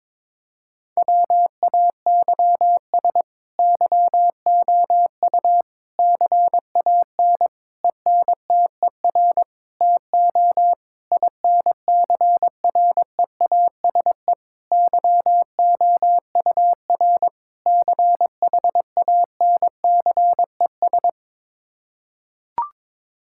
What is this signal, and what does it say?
Morse code, heard as a keyed tone, about 22 wpm, 700 Hz. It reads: WAYS YOU CAN ENTER TO INCREASE YOUR CHANCES
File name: practice / 22wpm / Sets of 8 Words - 22wpm